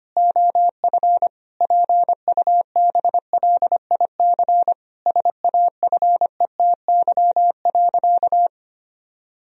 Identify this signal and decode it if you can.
Morse code, heard as a keyed tone, about 25 wpm, 700 Hz. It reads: OF PUBLIC SAFETY.
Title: Sentences from Top 1000 Words - Review Part 4 - 25wpm